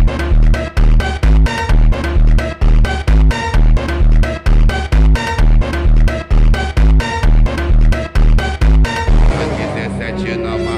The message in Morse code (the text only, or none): none